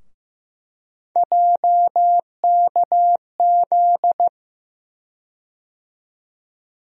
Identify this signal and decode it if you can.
Morse code, heard as a keyed tone, about 15 wpm, 700 Hz. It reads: JKZ
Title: USA Call Sign Postfixes - 3 Letters 15wpm